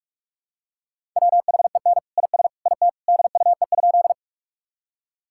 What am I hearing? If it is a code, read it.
Morse code, heard as a keyed tone, about 46 wpm, 700 Hz. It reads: WHEN IS IT DUE?
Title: Sentences from Top 1000 Words - Review Part 6 - 45wpm